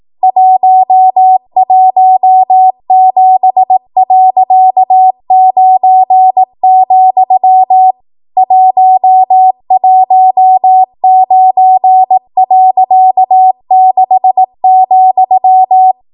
Morse code, heard as a keyed tone, about 18 wpm, 750 Hz. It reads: 117.9, 119.6,
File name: Archive / 18 WPM / 18WPM